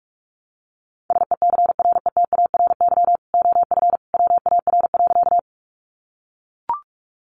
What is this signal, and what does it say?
Morse code, heard as a keyed tone, about 44 wpm, 700 Hz. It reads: SECRETARY OF WAR.
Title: Sentences from Top 1300 Words 45wpm